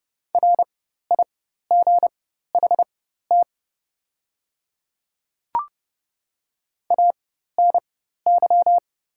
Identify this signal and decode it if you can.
Morse code, heard as a keyed tone, about 30 wpm, spaced as for 14 wpm, 700 Hz. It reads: RIGHT ANY
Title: 4x Character Spacing - Top 300 Words - 30wpm